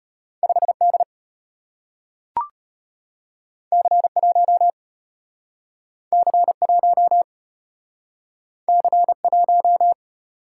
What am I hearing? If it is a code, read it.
Morse code, heard as a keyed tone, about 34 wpm, 690 Hz. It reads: 5D C1 C1 C1